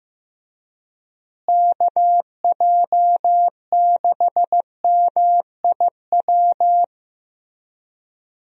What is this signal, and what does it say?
Morse code, heard as a keyed tone, about 15 wpm, 700 Hz. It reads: KJ6MIW
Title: USA Call Signs - 6 Characters - Repeated 2x - 15wpm